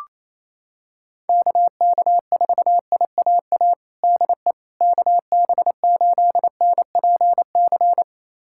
Morse code, heard as a keyed tone, about 28 wpm, 700 Hz. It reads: KK4IAA DE KB8NPC
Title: USA Callsign DE Callsign - 6 Characters - 28wpm